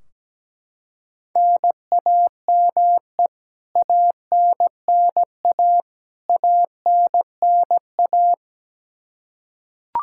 Morse code, heard as a keyed tone, about 17 wpm, 700 Hz. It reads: NAME ANNA ANNA
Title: Practice QSO - Exchange 1 - Beginning - 17wpm